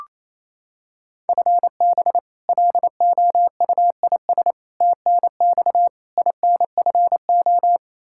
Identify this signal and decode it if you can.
Morse code, heard as a keyed tone, about 28 wpm, 700 Hz. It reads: FB LOUIS TNX INFO
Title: Practice QSO - Exchange 3 - More Info - 28wpm